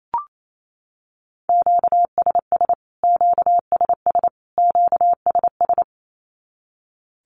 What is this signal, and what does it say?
Morse code, heard as a keyed tone, about 28 wpm, 700 Hz. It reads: QSS QSS QSS
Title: Q-Codes Repeated 3x 28wpm